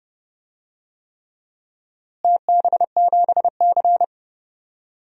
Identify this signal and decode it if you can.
Morse code, heard as a keyed tone, about 30 wpm, 700 Hz. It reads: TB7C
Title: DXCC-100 Call Signs - 4 Characters - Repeated 3x - 30wpm